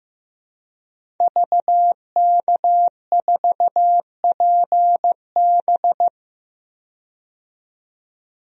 Morse code, heard as a keyed tone, about 15 wpm, 700 Hz. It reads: VK4PB
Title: DXCC-100 Call Signs - 5 Characters - 15wpm